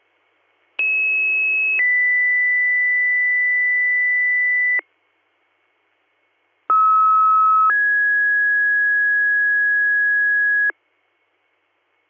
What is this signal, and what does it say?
Two-tone page, heard as a one-second tone, then a three-second tone, about 2600 then 2000 Hz, about 1290 then 1650 Hz. It is heard from 0.8 to 10.7 s.